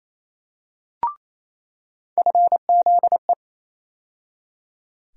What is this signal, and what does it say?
Morse code, heard as a keyed tone, about 28 wpm, 700 Hz. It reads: FZE
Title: USA Call Sign Postfixes - 3 Letters 28wpm